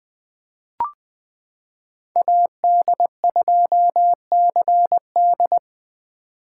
Morse code, heard as a keyed tone, about 20 wpm, 700 Hz. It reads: AD2CD
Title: 2023 POTA Hunter Call Signs - 20wpm